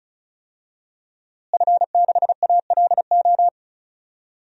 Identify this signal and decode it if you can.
Morse code, heard as a keyed tone, about 36 wpm, 700 Hz. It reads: F6ALO
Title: DXCC-100 Call Signs - 5 Characters - Repeated 3x - 35wpm